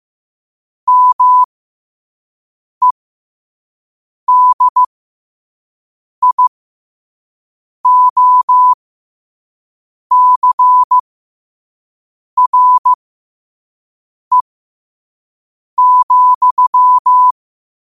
Morse code, heard as a keyed tone, about 15 wpm, spaced as for 5 wpm, 1000 Hz. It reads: MEDIOCRE,
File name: qotdfiles / 5wpm